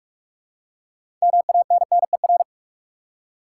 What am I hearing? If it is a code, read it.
Morse code, heard as a keyed tone, about 42 wpm, 700 Hz. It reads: MANNER
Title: Six Letter Words - 45wpm